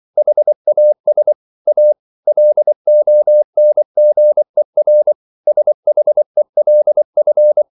Morse code, heard as a keyed tone, about 24 wpm, 600 Hz. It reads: HAS A LONGER SHELF
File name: CW / Practice / EnglishQuotations0224